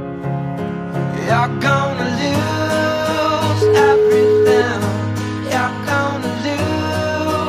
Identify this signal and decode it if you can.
Ringback tone (Europe, 425 Hz), with noise.